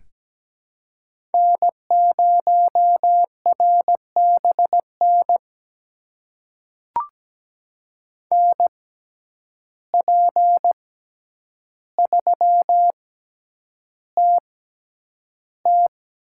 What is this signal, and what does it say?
Morse code, heard as a keyed tone, about 17 wpm, spaced as for 9 wpm, 700 Hz. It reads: N0RBN NP3TT
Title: USA Call Signs - 5 Characters - 6x Character Spacing - 17wpm